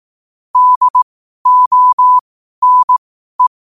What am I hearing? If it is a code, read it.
Morse code, heard as a keyed tone, about 18 wpm, spaced as for 12 wpm, 1000 Hz. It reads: DONE